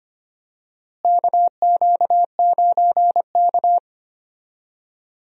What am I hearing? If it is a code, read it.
Morse code, heard as a keyed tone, about 25 wpm, 700 Hz. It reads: KQ9K
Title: USA Call Signs - 4 Characters - Repeated 2x - 25wpm